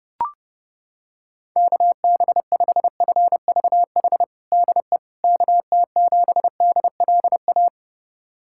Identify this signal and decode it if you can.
Morse code, heard as a keyed tone, about 30 wpm, 700 Hz. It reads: KB5FVH DE KT7DLA
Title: USA Callsign DE Callsign 30wpm